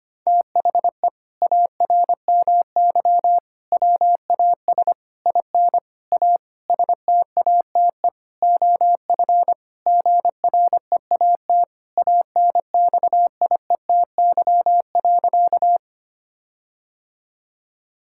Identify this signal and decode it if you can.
Morse code, heard as a keyed tone, about 25 wpm, 700 Hz. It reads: THE ARMY WAS IN A STATE OF GREAT ANXIETY.
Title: Sentences from Top 1600 Words 25wpm